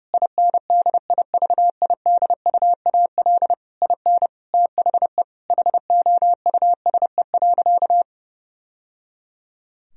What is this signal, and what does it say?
Morse code, heard as a keyed tone, about 30 wpm, 700 Hz. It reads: INDIVIDUAL IN THE HOUSE.
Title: Sentences from Top 1000 Words - Review Part 4 - 30wpm